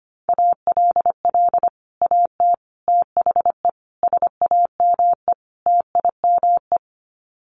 Morse code, heard as a keyed tone, about 25 wpm, 700 Hz. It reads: ALL AT THE SAME TIME
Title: Sets of 5 Words Encore2 25wpm